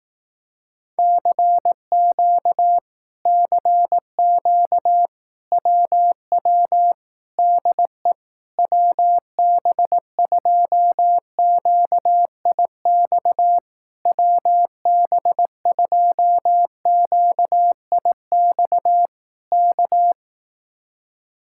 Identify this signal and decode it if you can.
Morse code, heard as a keyed tone, about 18 wpm, 700 Hz. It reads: CQ CQ WW DE WB2QIX WB2QIX K
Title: Speed-Racing Common QSO Phrases Encore - 15wpm